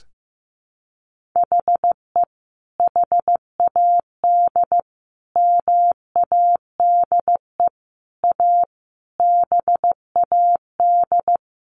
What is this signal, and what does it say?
Morse code, heard as a keyed tone, about 14 wpm, 700 Hz. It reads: HE HAD MADE A BAD
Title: Sentences from Top 1500 Words 15wpm